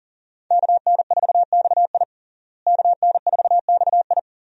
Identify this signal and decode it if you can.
Morse code, heard as a keyed tone, about 40 wpm, 700 Hz. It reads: KN4XI KN4XI
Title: USA Call Signs - 5 Characters - Repeated 3x - 40wpm